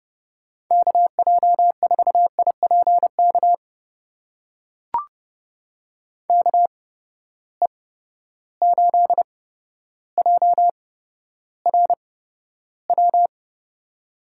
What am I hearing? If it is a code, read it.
Morse code, heard as a keyed tone, about 30 wpm, spaced as for 13 wpm, 700 Hz. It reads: KJ4IPK KE8JRW
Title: USA Call Signs - 8x Character Spacing - 30wpm